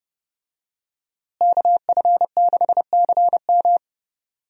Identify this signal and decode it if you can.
Morse code, heard as a keyed tone, about 32 wpm, 700 Hz. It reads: KF6CM
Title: USA Call Signs - 5 Characters - Repeated 2x - 30wpm